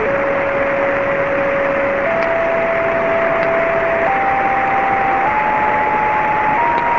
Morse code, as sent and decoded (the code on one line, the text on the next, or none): none